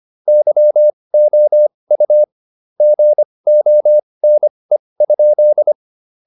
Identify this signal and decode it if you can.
Morse code, heard as a keyed tone, about 25 wpm, spaced as for 20 wpm, 600 Hz. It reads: YOU GONE?